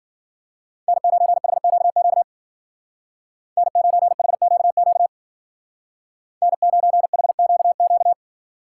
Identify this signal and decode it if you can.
Morse code, heard as a keyed tone, about 49 wpm, 700 Hz. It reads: N9HQQ N9HQQ N9HQQ
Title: Speed Racing - USA 1x3 Call Signs 40wpm